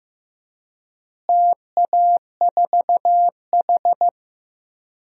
Morse code, heard as a keyed tone, about 15 wpm, 700 Hz. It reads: TA4H
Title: DXCC-100 Call Signs - 4 Characters - Repeated 3x - 15wpm